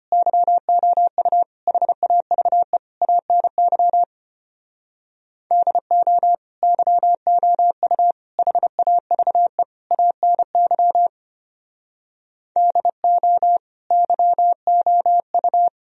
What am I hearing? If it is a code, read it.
Morse code, heard as a keyed tone, about 32 wpm, 700 Hz. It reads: YOU HAVE ANY DO YOU HAVE ANY DO YOU